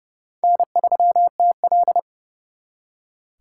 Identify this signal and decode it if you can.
Morse code, heard as a keyed tone, about 35 wpm, 700 Hz. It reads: N3TL
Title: Speed Racing - USA 1x2 Call Signs 30wpm